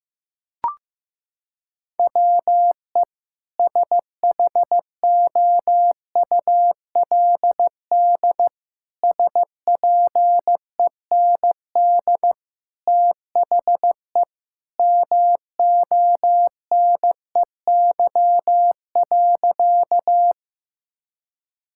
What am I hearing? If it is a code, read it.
Morse code, heard as a keyed tone, about 15 wpm, 700 Hz. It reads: WE SHOULD SPEND THE MONEY.
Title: Sentences from Top 1000 Words - Review Part 5 - 15wpm